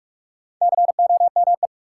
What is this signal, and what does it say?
Morse code, heard as a keyed tone, about 46 wpm, 700 Hz. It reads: COME
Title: Sets of 2 Words - New Word Drill - 45wpm